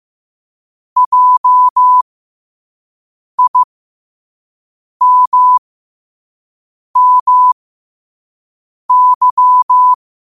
Morse code, heard as a keyed tone, about 15 wpm, spaced as for 5 wpm, 1000 Hz. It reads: JIMMY
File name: qotdfiles / 5wpm